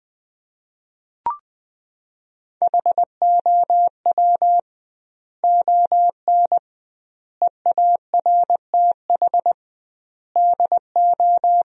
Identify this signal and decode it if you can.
Morse code, heard as a keyed tone, about 21 wpm, 700 Hz. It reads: HOW ON EARTH DO
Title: 2x Word-Spacing - Sentences from Top 200 Words - 20wpm